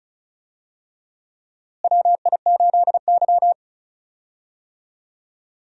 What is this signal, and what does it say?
Morse code, heard as a keyed tone, about 36 wpm, 700 Hz. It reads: WI8Y
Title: USA Call Signs - 4 Characters - Repeated 3x - 35wpm